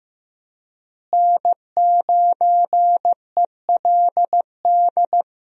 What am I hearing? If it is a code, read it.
Morse code, heard as a keyed tone, about 15 wpm, 700 Hz. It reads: N9ELD
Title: USA Call Signs - 5 Characters - Repeated 2x - 15wpm